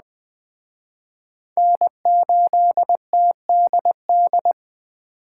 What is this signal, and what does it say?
Morse code, heard as a keyed tone, about 20 wpm, 700 Hz. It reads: N8TDD
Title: Speed Racing - USA 1x3 Call Signs 17wpm